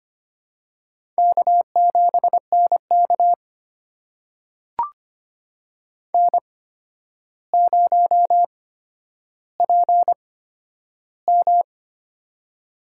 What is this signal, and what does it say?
Morse code, heard as a keyed tone, about 25 wpm, spaced as for 11 wpm, 700 Hz. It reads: K7NK N0PM
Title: USA Call Signs - 4 Characters - 8x Character Spacing - 25wpm